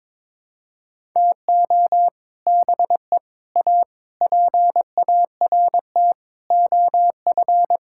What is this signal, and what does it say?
Morse code, heard as a keyed tone, about 22 wpm, 700 Hz. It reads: TO BE A PART OF
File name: practice / 22wpm / Sets of 6 Words - 22wpm